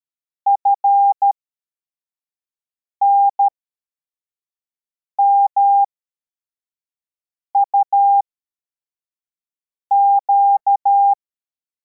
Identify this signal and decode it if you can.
Morse code, heard as a keyed tone, about 13 wpm, spaced as for 4 wpm, 800 Hz. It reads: FNMUQ
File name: formation / cw / Text74